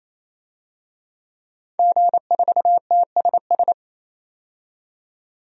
Morse code, heard as a keyed tone, about 28 wpm, 700 Hz. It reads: G4TSS
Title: DXCC-100 Call Signs - 5 Characters - 28wpm